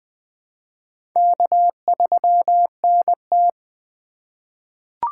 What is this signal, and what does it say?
Morse code, heard as a keyed tone, about 21 wpm, 700 Hz. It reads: K3NT
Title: USA Call Signs - 4 Characters - 3x Character Spacing - 20wpm